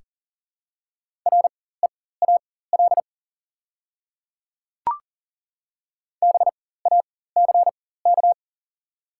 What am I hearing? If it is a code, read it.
Morse code, heard as a keyed tone, about 40 wpm, spaced as for 19 wpm, 700 Hz. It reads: REAL BACK
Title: 4x Character Spacing - Top 500 Words - 40wpm